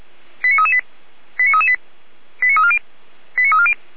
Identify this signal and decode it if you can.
Five-tone selective call (ZVEI), heard as quick runs of five tones, tones of about 70 ms. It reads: 89338 89338 89340 89340
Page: ZVEI